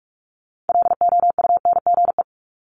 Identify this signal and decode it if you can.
Morse code, heard as a keyed tone, about 46 wpm, 700 Hz. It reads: LOUNGE
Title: Multi-letter Phonemes Repeated 3x 45wpm